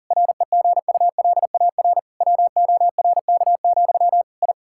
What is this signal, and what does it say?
Morse code, heard as a keyed tone, about 40 wpm, 700 Hz. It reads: REGULAR WORK, I
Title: Sentences from Top 1000 Words - Review Part 6 - 40wpm